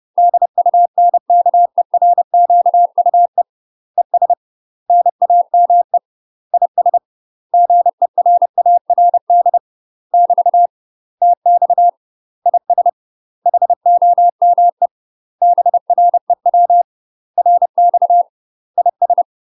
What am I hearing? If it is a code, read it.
Morse code, heard as a keyed tone, about 30 wpm, 700 Hz. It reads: DUNKERQUE ES NAME IS GERARD <BT> TX IS HOME BREW RX IS
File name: UK MASTER QSOs TRAINING / 30WPM / UK MASTER QSOs 0045